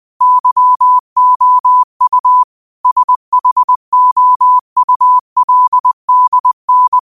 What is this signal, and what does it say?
Morse code, heard as a keyed tone, about 20 wpm, 1000 Hz. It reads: YOU SHOULDN
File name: qotdfiles / 20wpm